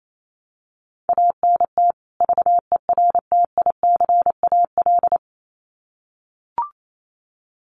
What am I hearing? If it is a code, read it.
Morse code, heard as a keyed tone, about 28 wpm, 700 Hz. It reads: ANT VERTICAL
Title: Common QSO Phrases 28wpm